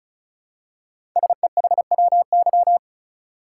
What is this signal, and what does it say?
Morse code, heard as a keyed tone, about 36 wpm, 700 Hz. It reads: SEHWY
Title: Code Groups - 5 Letters - 35wpm